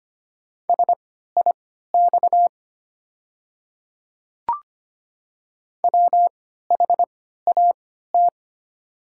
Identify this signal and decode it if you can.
Morse code, heard as a keyed tone, about 25 wpm, spaced as for 14 wpm, 690 Hz. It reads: SIX WHAT
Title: 3x Character Spacing - Top 500 Words - 25wpm